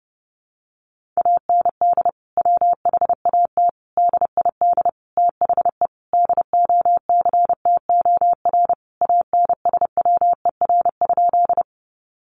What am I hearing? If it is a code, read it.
Morse code, heard as a keyed tone, about 30 wpm, 700 Hz. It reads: AND WHAT DID THE DOCTOR ANSWER?
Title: Sentences from Top 1500 Words 30wpm